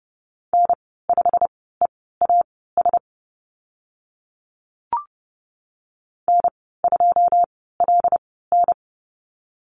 Morse code, heard as a keyed tone, about 30 wpm, spaced as for 17 wpm, 700 Hz. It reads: N5EAS N2LN
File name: Lesson 15g - Number 5 - Call Signs 30wpm